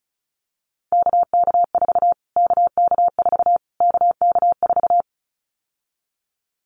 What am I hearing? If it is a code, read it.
Morse code, heard as a keyed tone, about 34 wpm, 700 Hz. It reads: KK4 KK4 KK4
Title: Callsign Prefixes for Top 25 DXCC - Repeated 3x - 35wpm